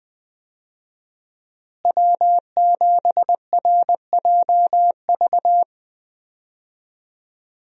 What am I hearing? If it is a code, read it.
Morse code, heard as a keyed tone, about 20 wpm, 700 Hz. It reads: W7RJV
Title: USA Call Signs - 5 Characters - Repeated 3x - 20wpm